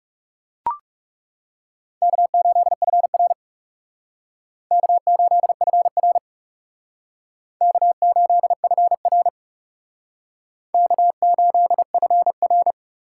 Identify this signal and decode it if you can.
Morse code, heard as a keyed tone, about 38 wpm, 700 Hz. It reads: K8FR K8FR K8FR K8FR